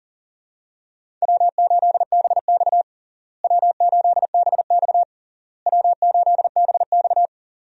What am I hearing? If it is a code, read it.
Morse code, heard as a keyed tone, about 40 wpm, 700 Hz. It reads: W8BX W8BX W8BX
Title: USA Call Signs - 4 Characters - Repeated 3x - 40wpm